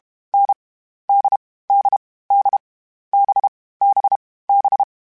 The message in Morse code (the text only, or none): N D D D B B B